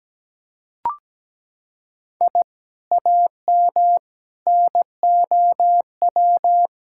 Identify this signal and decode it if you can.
Morse code, heard as a keyed tone, about 17 wpm, 700 Hz. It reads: I AM NOW
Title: Sentences from Top 1000 Words - Review Part 6 - 17wpm